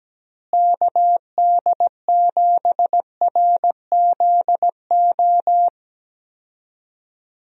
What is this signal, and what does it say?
Morse code, heard as a keyed tone, about 17 wpm, 700 Hz. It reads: KD7RZO
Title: USA Call Signs - 6 Characters - Repeated 3x - 17wpm